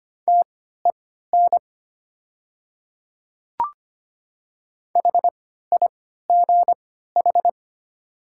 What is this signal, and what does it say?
Morse code, heard as a keyed tone, about 25 wpm, spaced as for 14 wpm, 700 Hz. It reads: TEN HIGH
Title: 3x Character Spacing - Top 500 Words - 25wpm